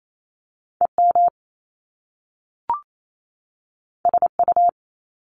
Morse code, heard as a keyed tone, about 28 wpm, 700 Hz. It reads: EM SU
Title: Top Two Letter Combinations in Words 28wpm